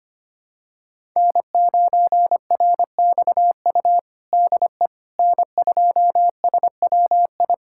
Morse code, heard as a keyed tone, about 25 wpm, 700 Hz. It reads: N9RXU DE N2SWI
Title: USA Callsign DE Callsign - 5 Characters - 25wpm